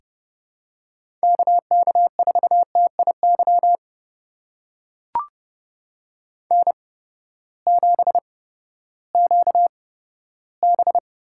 Morse code, heard as a keyed tone, about 30 wpm, spaced as for 15 wpm, 700 Hz. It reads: KK4TIY N7QB
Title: USA Call Signs - 8x Character Spacing - 30wpm